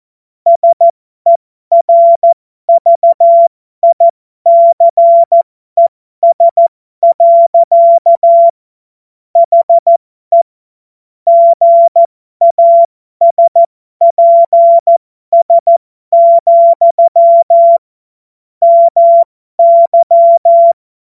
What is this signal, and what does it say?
Morse code, heard as a keyed tone, about 14 wpm, spaced as for 12 wpm, 680 Hz. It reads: SERVICES. HE GASPS, MY
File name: Morsekurs / Witze